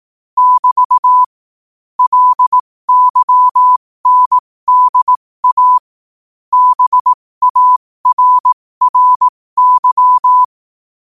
Morse code, heard as a keyed tone, about 18 wpm, spaced as for 15 wpm, 1000 Hz. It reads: <BT> LYNDA BARRY
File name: qotdfiles / 15wpm